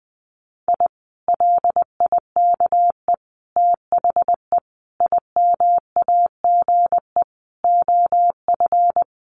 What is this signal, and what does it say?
Morse code, heard as a keyed tone, about 20 wpm, 700 Hz. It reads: I LIKE THE IMAGE OF
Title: Sentences from Top 700 Words 20wpm